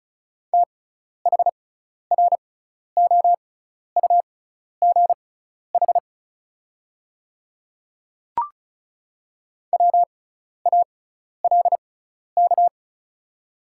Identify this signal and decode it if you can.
Morse code, heard as a keyed tone, about 36 wpm, spaced as for 12 wpm, 690 Hz. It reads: THROUGH WALK
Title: 6x Character Spacing - Top 300 Words - 35wpm